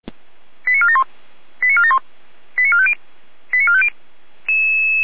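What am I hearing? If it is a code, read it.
Five-tone selective call (ZVEI), heard as quick runs of five tones, tones of about 70 ms. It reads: 89471 89471 89450 89450
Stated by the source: ZVEI five-tone call 88251 received on an SDR.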